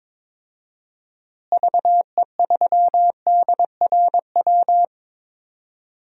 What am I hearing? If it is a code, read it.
Morse code, heard as a keyed tone, about 21 wpm, 700 Hz. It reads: VE3DRW
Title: DXCC-100 Call Signs - 6 Characters - Repeated 3x - 22wpm